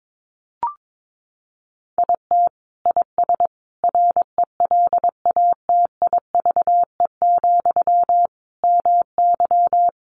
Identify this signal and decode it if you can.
Morse code, heard as a keyed tone, about 22 wpm, 700 Hz. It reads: IT IS RELATIVE, MY
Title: Sentences from Top 1400 Words 22wpm